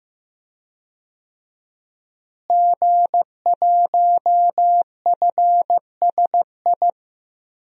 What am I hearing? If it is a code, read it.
Morse code, heard as a keyed tone, about 15 wpm, 700 Hz. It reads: G1FSI
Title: DXCC-100 Call Signs - 5 Characters - Repeated 2x - 15wpm